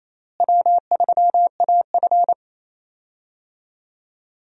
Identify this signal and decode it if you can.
Morse code, heard as a keyed tone, about 28 wpm, 700 Hz. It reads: W3AF
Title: USA Call Signs - 4 Characters - Repeated 2x - 28wpm